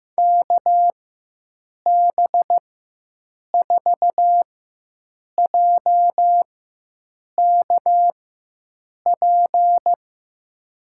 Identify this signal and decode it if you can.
Morse code, heard as a keyed tone, about 15 wpm, spaced as for 7 wpm, 700 Hz. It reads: KB4JKP
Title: USA Call Signs - 4x Character Spacing - 15wpm